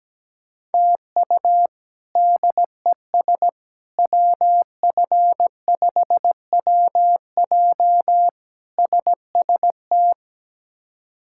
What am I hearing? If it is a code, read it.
Morse code, heard as a keyed tone, about 17 wpm, 700 Hz. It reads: TU DES WF5WJ SST
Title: K1USN SST QSO Practice - 17wpm